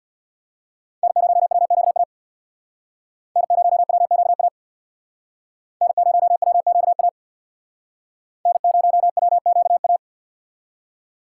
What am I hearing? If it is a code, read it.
Morse code, heard as a keyed tone, about 62 wpm, 700 Hz. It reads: N0WQA N0WQA N0WQA N0WQA
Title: Speed Racing - USA 1x3 Call Signs 50wpm